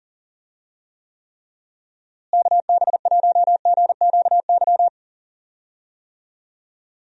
Morse code, heard as a keyed tone, about 40 wpm, 700 Hz. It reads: KB1GQY
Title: USA Call Signs - 6 Characters - Repeated 3x - 40wpm